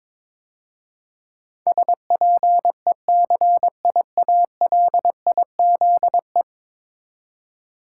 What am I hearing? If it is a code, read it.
Morse code, heard as a keyed tone, about 22 wpm, 700 Hz. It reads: SPECIALIZE
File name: practice / 22wpm / 10 Letter Words - First 3 Letters Given - 22wpm